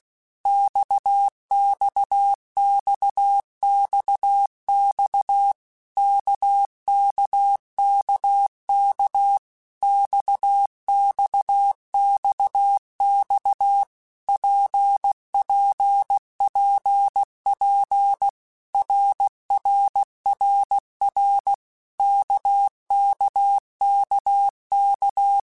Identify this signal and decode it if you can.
Morse code, heard as a keyed tone, about 16 wpm, 800 Hz. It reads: XXXXX KKKK XXXX PPPP RRRR KKKK
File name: Basic Letter Groups By7Groups / RKPX / RKPX@15